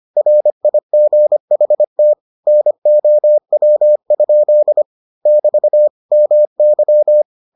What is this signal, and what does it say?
Morse code, heard as a keyed tone, about 25 wpm, 600 Hz. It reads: RIGHT NOW? <BT> MY